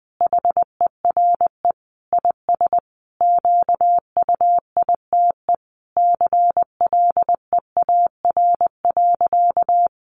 Morse code, heard as a keyed tone, about 20 wpm, 700 Hz. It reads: HERE IS QUITE CLEAR.
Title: Sentences from Top 1100 Words 20wpm